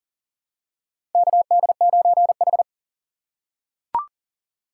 Morse code, heard as a keyed tone, about 40 wpm, 700 Hz. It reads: KD9H
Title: USA Call Signs - 3x Character Spacing - 40wpm